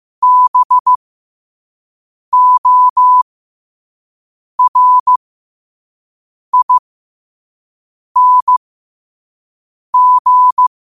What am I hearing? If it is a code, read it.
Morse code, heard as a keyed tone, about 15 wpm, spaced as for 5 wpm, 1000 Hz. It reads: BORING
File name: qotdfiles / 5wpm